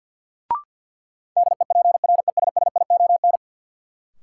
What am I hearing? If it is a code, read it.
Morse code, heard as a keyed tone, about 53 wpm, 700 Hz. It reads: DEPRESSION
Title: Sentences from Top 1300 Words - New Word Drill Alt - 50wpm